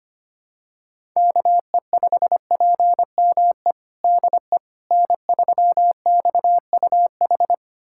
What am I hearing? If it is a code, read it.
Morse code, heard as a keyed tone, about 25 wpm, 700 Hz. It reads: KE5PME DE N3XUH
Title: USA Callsign DE Callsign 25wpm